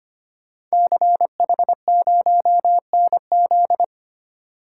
Morse code, heard as a keyed tone, about 25 wpm, 700 Hz. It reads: CH0NZ
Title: Code Groups - 5 Letter:Numbers - 25wpm